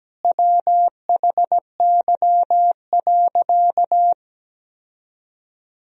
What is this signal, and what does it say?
Morse code, heard as a keyed tone, about 17 wpm, 700 Hz. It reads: WHY.
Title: Sentences from Top 1000 Words - Review Part 3 - 17wpm